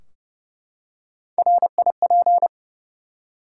Morse code, heard as a keyed tone, about 29 wpm, 700 Hz. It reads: RIP
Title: Three Letter Words - 30wpm